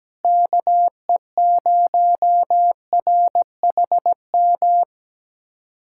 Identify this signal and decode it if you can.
Morse code, heard as a keyed tone, about 17 wpm, 700 Hz. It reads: KE0RHM
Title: USA Callsign DE Callsign - 6 Characters - 17wpm